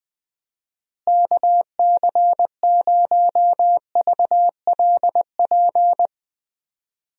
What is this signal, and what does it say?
Morse code, heard as a keyed tone, about 20 wpm, 700 Hz. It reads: KC0VLP
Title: USA Call Signs - 6 Characters - Repeated 2x - 20wpm